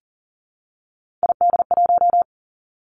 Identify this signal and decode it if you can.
Morse code, heard as a keyed tone, about 38 wpm, 700 Hz. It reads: ID1
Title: Three Challenging Letter-Number Combinations 40wpm